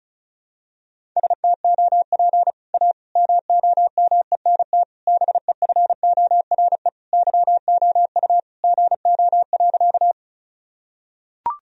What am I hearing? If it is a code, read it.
Morse code, heard as a keyed tone, about 34 wpm, 700 Hz. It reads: STOP A MOMENT BEFORE YOU GO.
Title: Sentences from Top 1000 Words - Review Part 1 - 35wpm